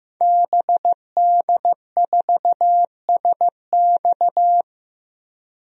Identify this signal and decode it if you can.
Morse code, heard as a keyed tone, about 15 wpm, 700 Hz. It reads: BD4SX
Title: DXCC-100 Call Signs - 5 Characters - 15wpm